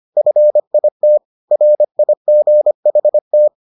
Morse code, heard as a keyed tone, about 25 wpm, 600 Hz. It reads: FIT RIGHT